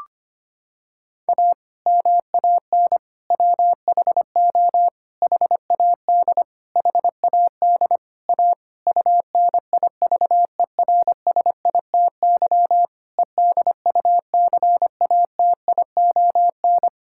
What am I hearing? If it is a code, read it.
Morse code, heard as a keyed tone, about 25 wpm, 700 Hz. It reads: A MAN WHO HAD HAD A UNIVERSITY EDUCATION
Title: Sentences from Top 1000 Words - Review Part 6 - 25wpm